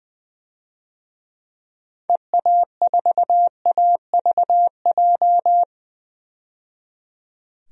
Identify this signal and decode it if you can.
Morse code, heard as a keyed tone, about 21 wpm, 700 Hz. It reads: EA4AVJ
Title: DXCC-100 Call Signs - 6 Characters - 20wpm